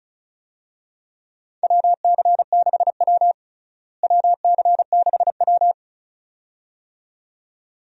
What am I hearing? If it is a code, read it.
Morse code, heard as a keyed tone, about 36 wpm, 700 Hz. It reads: WC6W WC6W
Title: USA Call Signs - 4 Characters - Repeated 2x - 35wpm